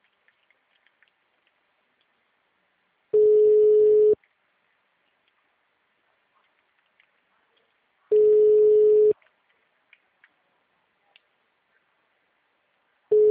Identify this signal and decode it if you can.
Ringback tone (Europe, 425 Hz).
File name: out- -2017